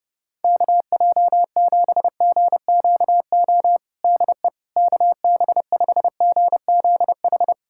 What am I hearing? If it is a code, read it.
Morse code, heard as a keyed tone, about 29 wpm, 700 Hz. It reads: KJ7GQO DE KB5GZH
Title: USA Callsign DE Callsign - 6 Characters - 30wpm